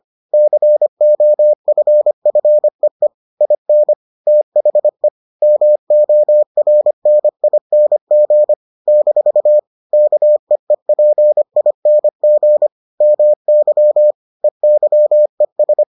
Morse code, heard as a keyed tone, about 25 wpm, 600 Hz. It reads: COFFEE IN THE MORNING - KEEPING MY EYES